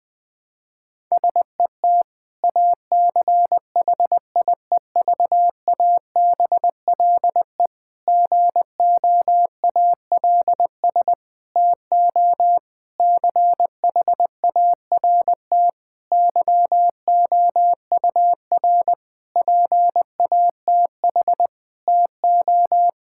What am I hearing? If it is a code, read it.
Morse code, heard as a keyed tone, about 20 wpm, 700 Hz. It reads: SET ACHIEVABLE GOALS TO CHART YOUR PATH TO
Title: Difficult Sentences with Hints - 20wpm